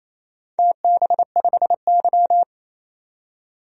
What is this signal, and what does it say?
Morse code, heard as a keyed tone, about 28 wpm, 700 Hz. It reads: TB5Y
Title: DXCC-100 Call Signs - 4 Characters - 28wpm